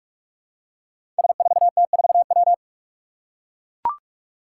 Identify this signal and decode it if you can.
Morse code, heard as a keyed tone, about 44 wpm, 700 Hz. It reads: S4T4W
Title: Five Challenging Letter-Number Combinations 45wpm